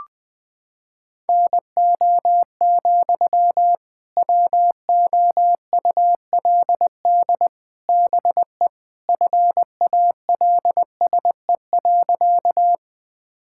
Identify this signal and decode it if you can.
Morse code, heard as a keyed tone, about 20 wpm, 700 Hz. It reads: NO, WOULD BE FALSE.
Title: Sentences from Top 1600 Words 20wpm